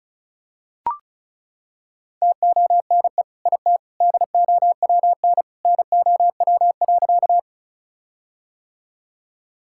Morse code, heard as a keyed tone, about 34 wpm, 700 Hz. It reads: TONE IT DOWN NOW.
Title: Sentences from Top 1600 Words 35wpm